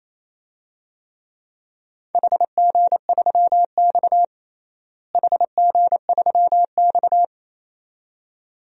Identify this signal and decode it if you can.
Morse code, heard as a keyed tone, about 28 wpm, 700 Hz. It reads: HG3X HG3X
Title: DXCC-100 Call Signs - 4 Characters - Repeated 2x - 28wpm